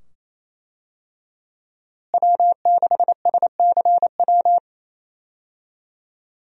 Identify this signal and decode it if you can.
Morse code, heard as a keyed tone, about 28 wpm, 700 Hz. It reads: W6SCW
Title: USA Call Signs - 5 Characters - Repeated 2x - 28wpm